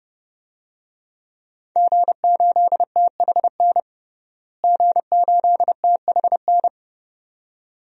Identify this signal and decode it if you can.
Morse code, heard as a keyed tone, about 30 wpm, 700 Hz. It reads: G8THN G8THN
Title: DXCC-100 Call Signs - 5 Characters - Repeated 2x - 30wpm